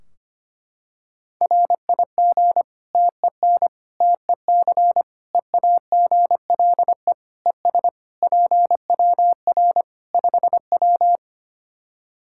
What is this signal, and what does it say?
Morse code, heard as a keyed tone, about 25 wpm, 700 Hz. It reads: RIG TEN TEC EAGLE ES PWR 5W
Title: Practice QSO - Exchange 2 - Info - 25wpm